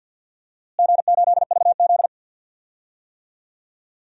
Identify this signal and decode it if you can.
Morse code, heard as a keyed tone, about 47 wpm, 700 Hz. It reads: G8VZ
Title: DXCC-100 Call Signs - 4 Characters - Repeated 3x - 50wpm